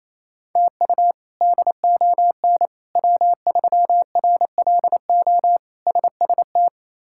Morse code, heard as a keyed tone, about 28 wpm, 700 Hz. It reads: TU DON W3RLO SST
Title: K1USN SST QSO Practice - 28wpm